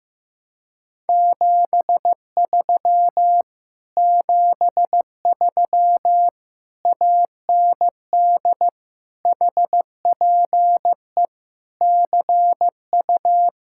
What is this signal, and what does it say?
Morse code, heard as a keyed tone, about 15 wpm, 700 Hz. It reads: 73 73 AND HPE CU
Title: Speed-Racing Common QSO Phrases Encore - 15wpm